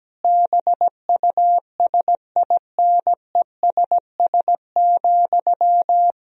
Morse code, heard as a keyed tone, about 17 wpm, 700 Hz. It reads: BUSINESS,
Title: Sentences from Top 1000 Words - Review Part 3 - 17wpm